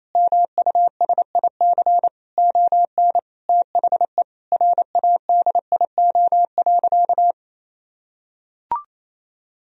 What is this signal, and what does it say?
Morse code, heard as a keyed tone, about 28 wpm, 700 Hz. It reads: MUSIC ON THE RADIO.
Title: Sentences from Top 1000 Words - Review Part 3 - 28wpm